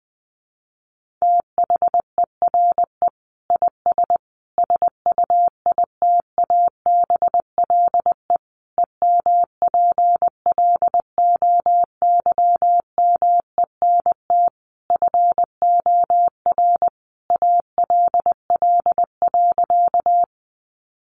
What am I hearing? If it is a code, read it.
Morse code, heard as a keyed tone, about 20 wpm, 700 Hz. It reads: THERE IS SUITABLE EMPLOYMENT FOR ALL.
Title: Sentences from Top 1500 Words 20wpm